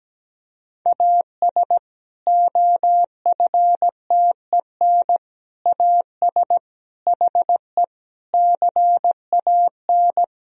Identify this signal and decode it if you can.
Morse code, heard as a keyed tone, about 17 wpm, 700 Hz. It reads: AS OFTEN AS HE CAN
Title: Sentences from Top 1000 Words - Review Part 1 - 17wpm